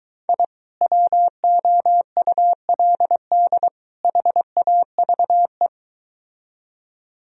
Morse code, heard as a keyed tone, about 23 wpm, 700 Hz. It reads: I WOULD HAVE